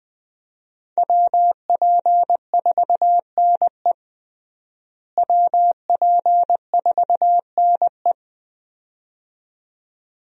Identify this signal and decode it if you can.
Morse code, heard as a keyed tone, about 21 wpm, 700 Hz. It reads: WP4NE WP4NE
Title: USA Call Signs - 5 Characters - Repeated 2x - 20wpm